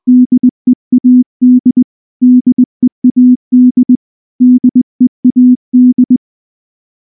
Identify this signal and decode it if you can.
Morse code, heard as a keyed tone, about 20 wpm, 250 Hz. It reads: DEAD DEAD DEAD
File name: morse-training-20WPM